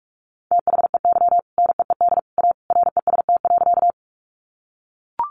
Morse code, heard as a keyed tone, about 44 wpm, 700 Hz. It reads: THEY NEED A REST.
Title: Sentences from Top 500 Words 45wpm